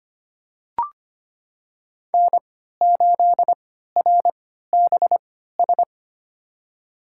Morse code, heard as a keyed tone, about 25 wpm, spaced as for 14 wpm, 700 Hz. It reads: N8RBS
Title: USA Call Signs - 5 Characters - 3x Character Spacing - 25wpm